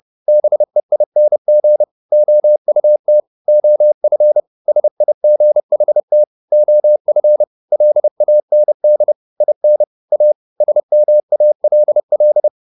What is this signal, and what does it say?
Morse code, heard as a keyed tone, about 30 wpm, 600 Hz. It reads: BEING OUT OF SIGHT OF LAND IN A SMALL
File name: the old man and the sea / the old man and the sea 1 30wpm